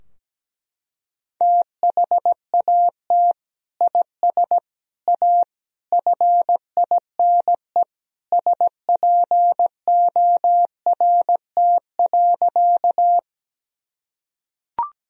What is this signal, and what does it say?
Morse code, heard as a keyed tone, about 17 wpm, 700 Hz. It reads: THAT IS A FINE SPORT.
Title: Sentences from Top 1000 Words - Review Part 4 - 17wpm